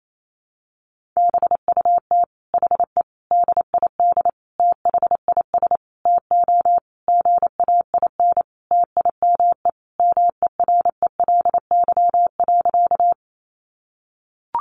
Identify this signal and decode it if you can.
Morse code, heard as a keyed tone, about 28 wpm, 700 Hz. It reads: BUT HE DID THIS TO GAIN TIME MERELY.
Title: Sentences from Top 1400 Words 28wpm